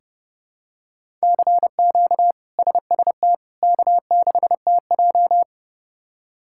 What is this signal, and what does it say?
Morse code, heard as a keyed tone, about 32 wpm, 700 Hz. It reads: CQ SST K6TJ
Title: K1USN SST QSO Practice - 30wpm